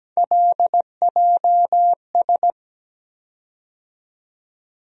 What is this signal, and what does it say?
Morse code, heard as a keyed tone, about 17 wpm, 700 Hz. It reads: LJS
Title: USA Call Sign Postfixes - 3 Letters 17wpm